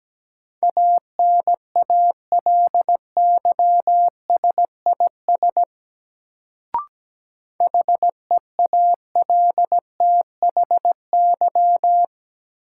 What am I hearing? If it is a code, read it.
Morse code, heard as a keyed tone, about 17 wpm, 700 Hz. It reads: ANALYSIS HEALTHY
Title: Sentences from Top 1400 Words - New Word Drill Alt - 17wpm